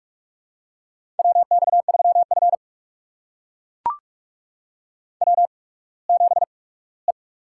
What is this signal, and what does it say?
Morse code, heard as a keyed tone, about 42 wpm, spaced as for 21 wpm, 700 Hz. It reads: WX3F W7E
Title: USA Call Signs - 8x Character Spacing - 45wpm